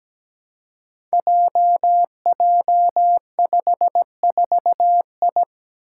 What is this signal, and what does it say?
Morse code, heard as a keyed tone, about 17 wpm, 700 Hz. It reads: JJ54I
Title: Five Challenging Letter-Number Combinations 17wpm